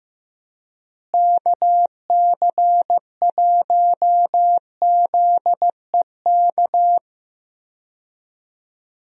Morse code, heard as a keyed tone, about 15 wpm, 700 Hz. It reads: KC1ZEK
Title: USA Call Signs - 6 Characters - Repeated 3x - 15wpm